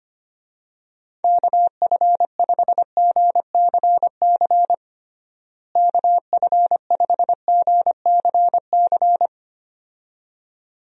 Morse code, heard as a keyed tone, about 25 wpm, 700 Hz. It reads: KF5GCC KF5GCC
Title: USA Call Signs - 6 Characters - Repeated 2x - 25wpm